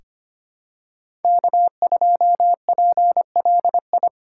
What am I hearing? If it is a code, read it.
Morse code, heard as a keyed tone, about 25 wpm, 700 Hz. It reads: K2PLI
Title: USA Call Signs - 5 Characters - 3x Character Spacing - 25wpm